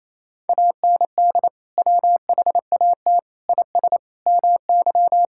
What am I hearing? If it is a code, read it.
Morse code, heard as a keyed tone, about 28 wpm, 700 Hz. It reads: AND WHAT IS MY
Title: Sentences from Top 1000 Words - Review Part 6 - 28wpm